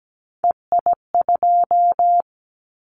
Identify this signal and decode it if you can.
Morse code, heard as a keyed tone, about 17 wpm, 700 Hz. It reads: EI2
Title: Callsign Prefixes for Top 50 DXCC - 17wpm